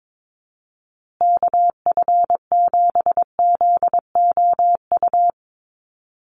Morse code, heard as a keyed tone, about 22 wpm, 700 Hz. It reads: KF7ZOU
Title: USA Call Signs Repeated 3x 22wpm